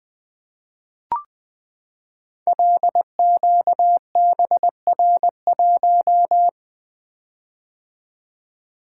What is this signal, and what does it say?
Morse code, heard as a keyed tone, about 20 wpm, 700 Hz. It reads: LQBR1
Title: Code Groups - 5 Letter:Numbers - 20wpm